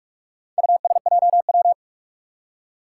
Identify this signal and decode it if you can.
Morse code, heard as a keyed tone, about 44 wpm, 700 Hz. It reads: USJW
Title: Four Challenging Letter-Number Combinations 45wpm